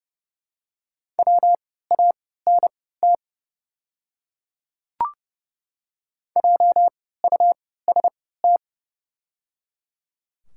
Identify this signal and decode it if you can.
Morse code, heard as a keyed tone, about 30 wpm, spaced as for 17 wpm, 700 Hz. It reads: WANT JUST
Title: 3x Character Spacing - Top 300 Words - 30wpm